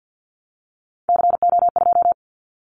Morse code, heard as a keyed tone, about 50 wpm, 700 Hz. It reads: CO2
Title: Callsign Prefixes for Top 100 DXCC - 50wpm